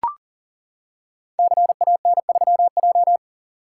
Morse code, heard as a keyed tone, about 42 wpm, 700 Hz. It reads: CAN3J
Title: Code Groups - 5 Letter:Numbers - 40wpm